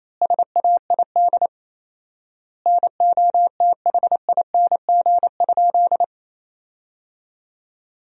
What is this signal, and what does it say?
Morse code, heard as a keyed tone, about 28 wpm, 700 Hz. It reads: SAID NOTHING?
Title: 4x Word-Spacing - Sentences from Top 500 Words - 28wpm